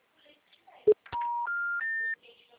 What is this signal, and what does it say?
Special information tone, with speech.
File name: out- -2032- - - .19366